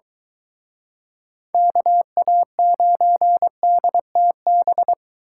Morse code, heard as a keyed tone, about 23 wpm, 700 Hz. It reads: KA9DTB